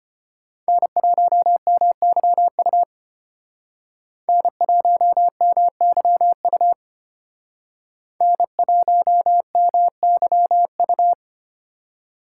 Morse code, heard as a keyed tone, about 32 wpm, 700 Hz. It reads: N1MYU N1MYU N1MYU